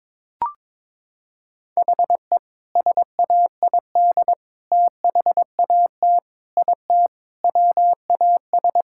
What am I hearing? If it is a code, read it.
Morse code, heard as a keyed tone, about 22 wpm, 700 Hz. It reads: HE SAID THAT IT WAS
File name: practice / 22wpm / Sentences with 2-Word Phrases - 22wpm